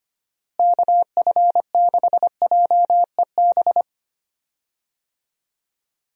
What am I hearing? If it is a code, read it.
Morse code, heard as a keyed tone, about 25 wpm, 700 Hz. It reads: KF6JEB
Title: USA Call Signs - 6 Characters - Repeated 2x - 25wpm